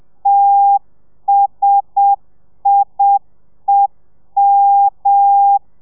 Morse code, heard as a keyed tone, about 7 wpm, 800 Hz. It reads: TSIEM
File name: telegrafia / i2tfj / Lez1 30